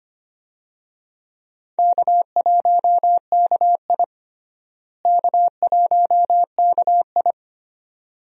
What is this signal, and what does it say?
Morse code, heard as a keyed tone, about 25 wpm, 700 Hz. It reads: K1KI K1KI
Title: USA Call Signs - 4 Characters - Repeated 2x - 25wpm